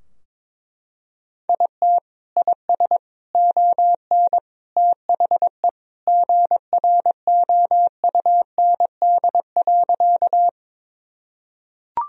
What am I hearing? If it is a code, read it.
Morse code, heard as a keyed tone, about 22 wpm, 700 Hz. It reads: IT IS ON THE GROUND.
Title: Sentences from Top 1000 Words - Review Part 2 - 22wpm